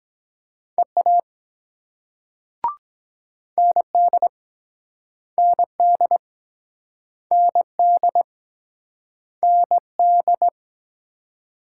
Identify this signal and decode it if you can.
Morse code, heard as a keyed tone, about 23 wpm, 700 Hz. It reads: EA ND ND ND ND